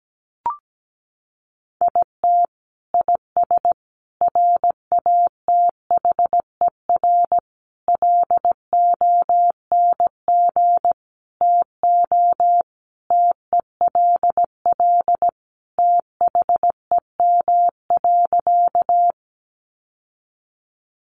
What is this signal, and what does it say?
Morse code, heard as a keyed tone, about 17 wpm, 700 Hz. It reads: IT IS RATHER LONG TO TELL THEM.
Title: Sentences from Top 700 Words 17wpm